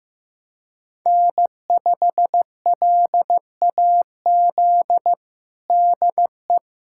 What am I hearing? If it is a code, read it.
Morse code, heard as a keyed tone, about 15 wpm, 700 Hz. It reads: N5LAZ DE
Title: USA Callsign DE Callsign 15wpm